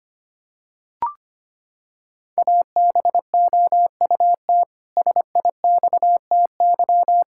Morse code, heard as a keyed tone, about 25 wpm, 700 Hz. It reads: ABOUT SIXTY
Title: Pangram Challenge v2 - 25wpm